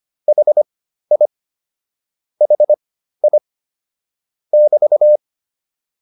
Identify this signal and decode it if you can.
Morse code, heard as a keyed tone, about 25 wpm, spaced as for 13 wpm, 600 Hz. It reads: HI HI <BT>
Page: intermediate-practice-files